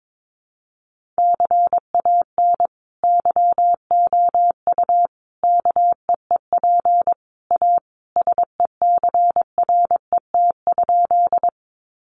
Morse code, heard as a keyed tone, about 22 wpm, 700 Hz. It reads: CAN YOU KEEP A SECRET?
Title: Sentences from Top 1200 Words 22wpm